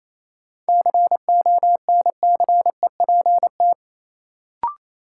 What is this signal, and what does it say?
Morse code, heard as a keyed tone, about 28 wpm, 700 Hz. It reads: CONCEPT
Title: Sentences from Top 1100 Words - New Word Drill - 28wpm